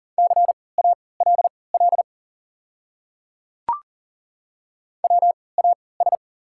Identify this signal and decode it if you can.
Morse code, heard as a keyed tone, about 40 wpm, spaced as for 23 wpm, 690 Hz. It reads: CALL WAS
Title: 3x Character Spacing - Top 200 Words - 40wpm